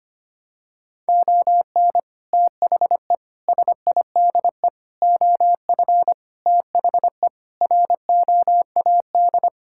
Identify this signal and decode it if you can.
Morse code, heard as a keyed tone, about 25 wpm, 700 Hz. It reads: ON THE SIDE OF THE ROAD
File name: practice / 25wpm / Sentences with 3-Word Phrases - 25wpm